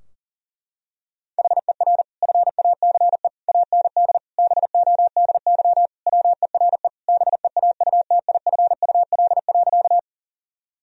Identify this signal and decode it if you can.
Morse code, heard as a keyed tone, about 40 wpm, 700 Hz. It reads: HER FACE AND BODY WERE BEAUTIFUL.
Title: Sentences from Top 1000 Words - Review Part 2 - 40wpm